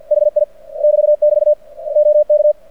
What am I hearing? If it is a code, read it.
Morse code, heard as a keyed tone, about 48 wpm, 600 Hz. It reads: HI <BT> <BT> OK
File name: archived / CRSnet